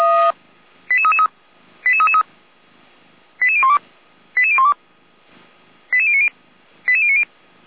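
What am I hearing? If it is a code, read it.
Five-tone selective call (ZVEI), heard as quick runs of five tones, tones of about 80 ms. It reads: 88333 88333 88012 88012 88090 88090
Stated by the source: ZVEI five-tone call received on SDR.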